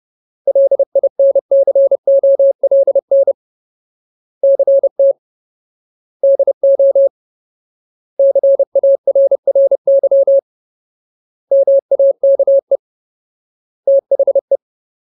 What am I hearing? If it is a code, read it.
Morse code, heard as a keyed tone, about 30 wpm, 550 Hz. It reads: LINCOLN CT DO CARRY MAKE THE
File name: Ultimate Word List / 30WPM Ultimate